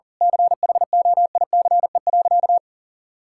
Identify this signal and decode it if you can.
Morse code, heard as a keyed tone, about 40 wpm, 700 Hz. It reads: CHOICE.
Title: Sentences from Top 1000 Words - Review Part 6 - 40wpm